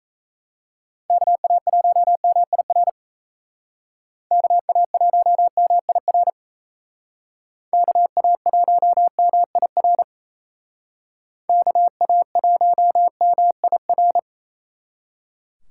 Morse code, heard as a keyed tone, about 34 wpm, 700 Hz. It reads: KA1MIR KA1MIR KA1MIR KA1MIR
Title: Speed Racing - USA 2x3 Call Signs 28wpm